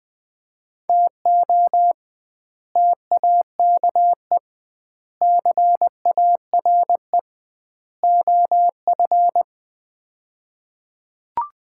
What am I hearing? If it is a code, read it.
Morse code, heard as a keyed tone, about 21 wpm, 700 Hz. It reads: TO TAKE CARE OF
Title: 2x Word-Spacing - Sets of 4 Words - 20wpm